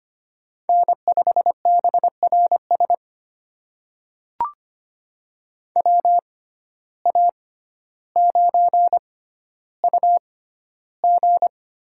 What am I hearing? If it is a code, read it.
Morse code, heard as a keyed tone, about 25 wpm, spaced as for 13 wpm, 700 Hz. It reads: N5BRS WA9UG
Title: USA Call Signs - 6x Character Spacing - 25wpm